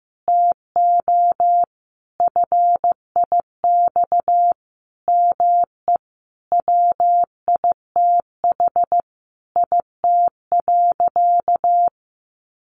Morse code, heard as a keyed tone, about 15 wpm, 700 Hz. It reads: TO FIX ME WITH IT.
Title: Sentences from Top 1400 Words 15wpm